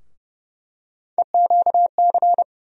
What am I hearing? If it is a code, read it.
Morse code, heard as a keyed tone, about 30 wpm, 700 Hz. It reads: EQC
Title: USA Call Sign Postfixes - 3 Letters 30wpm